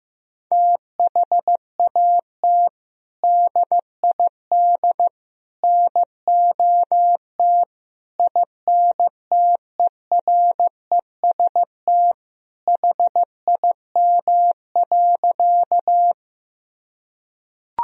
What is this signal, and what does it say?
Morse code, heard as a keyed tone, about 15 wpm, 700 Hz. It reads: THAT DID NOT INTEREST HIM.
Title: Sentences from Top 1000 Words - Review Part 2 - 15wpm